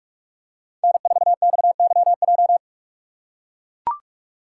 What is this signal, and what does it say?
Morse code, heard as a keyed tone, about 41 wpm, 700 Hz. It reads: N4XYJ
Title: USA Call Signs - 6x Character Spacing - 45wpm